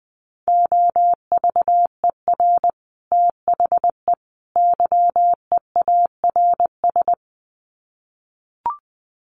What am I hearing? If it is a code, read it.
Morse code, heard as a keyed tone, about 20 wpm, 700 Hz. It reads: OVER THE YEARS
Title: Sets of 3 Words 20wpm